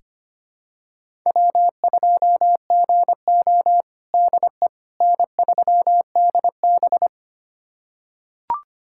Morse code, heard as a keyed tone, about 25 wpm, 700 Hz. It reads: W2GO DE N3DB
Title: USA Callsign DE Callsign - 4 Characters - 25wpm